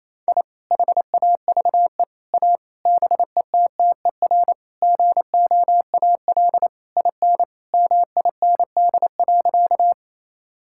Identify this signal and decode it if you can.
Morse code, heard as a keyed tone, about 28 wpm, 700 Hz. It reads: I HAVE A BETTER GOAL IN MIND.
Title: Sentences from Top 1000 Words - Review Part 5 - 28wpm